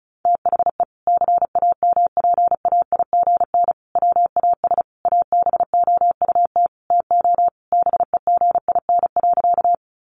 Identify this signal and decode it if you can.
Morse code, heard as a keyed tone, about 34 wpm, 700 Hz. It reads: THE CAMPAIGN WAS ABOUT TO BEGIN.
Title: Sentences from Top 1200 Words 35wpm